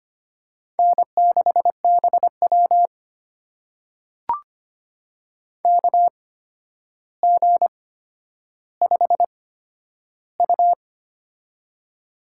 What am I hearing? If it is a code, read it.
Morse code, heard as a keyed tone, about 25 wpm, spaced as for 11 wpm, 700 Hz. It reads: N6BW KG5U
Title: USA Call Signs - 4 Characters - 8x Character Spacing - 25wpm